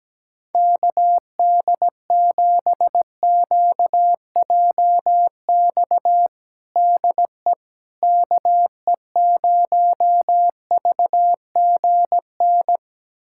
Morse code, heard as a keyed tone, about 17 wpm, 700 Hz. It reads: KD7QJX DE KE0VGN
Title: USA Callsign DE Callsign 17wpm